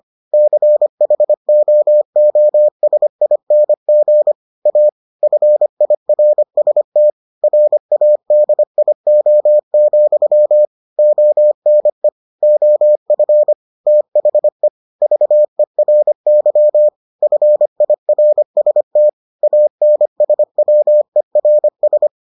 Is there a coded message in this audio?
Yes: Morse code from 0.3 to 22.1 s, heard as a keyed tone, about 25 wpm, 600 Hz. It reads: CHOOSING A FIRST RADIO, ONE OF THE VERY FIRST ANSWERS